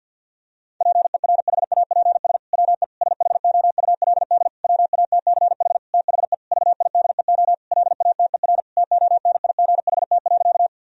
Morse code, heard as a keyed tone, about 51 wpm, 700 Hz. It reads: PERHAPS WE SHOULD WATCH THE VIDEO LATER TONIGHT.
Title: Sentences from Top 1000 Words - Review Part 4 - 50wpm